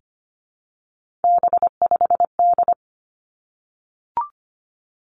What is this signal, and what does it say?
Morse code, heard as a keyed tone, about 25 wpm, 700 Hz. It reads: B5D
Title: Three Challenging Letter-Number Combinations 25wpm